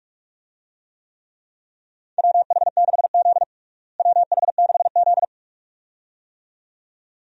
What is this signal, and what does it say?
Morse code, heard as a keyed tone, about 46 wpm, 700 Hz. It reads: WH6Z WH6Z
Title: USA Call Signs - 4 Characters - Repeated 2x - 45wpm